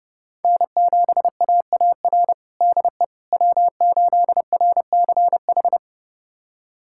Morse code, heard as a keyed tone, about 32 wpm, 700 Hz. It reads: N7AAR DE W8RCH
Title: USA Callsign DE Callsign - 5 Characters - 30wpm